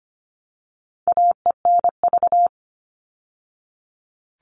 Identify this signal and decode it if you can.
Morse code, heard as a keyed tone, about 25 wpm, 700 Hz. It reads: AENV
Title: All Cut Numbers - 4-Digit 25wpm